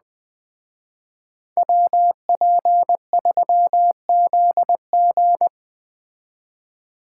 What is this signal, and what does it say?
Morse code, heard as a keyed tone, about 20 wpm, 700 Hz. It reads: WP3ZG
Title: Speed Racing - USA Call Signs 17wpm